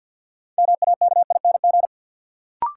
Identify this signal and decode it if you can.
Morse code, heard as a keyed tone, about 52 wpm, 700 Hz. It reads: MAKING
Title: Sentences from Top 1000 Words - Review Part 6 - New Word Drill - 50wpm